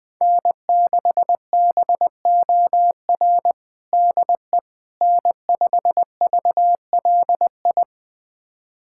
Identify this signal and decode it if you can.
Morse code, heard as a keyed tone, about 20 wpm, 700 Hz. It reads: N6BOR DE N5VLI
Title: USA Callsign DE Callsign - 5 Characters - 20wpm